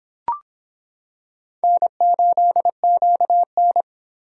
Morse code, heard as a keyed tone, about 25 wpm, 700 Hz. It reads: N8QN